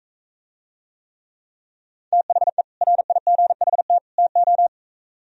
Morse code, heard as a keyed tone, about 45 wpm, 700 Hz. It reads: THE RIGHT TO